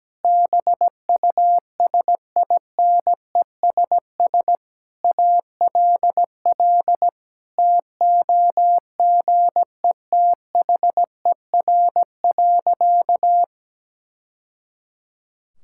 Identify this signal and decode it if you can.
Morse code, heard as a keyed tone, about 17 wpm, 700 Hz. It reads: BUSINESS ALL TOGETHER.
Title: Sentences from Top 1000 Words - Review Part 6 - 17wpm